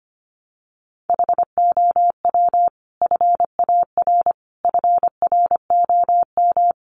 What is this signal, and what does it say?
Morse code, heard as a keyed tone, about 25 wpm, 700 Hz. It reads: HOW FAR FROM
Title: Sentences from Top 400 Words 25wpm